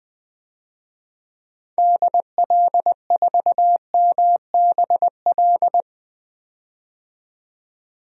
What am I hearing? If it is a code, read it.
Morse code, heard as a keyed tone, about 20 wpm, 700 Hz. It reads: DL4MBL
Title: DXCC-100 Call Signs - 6 Characters - 20wpm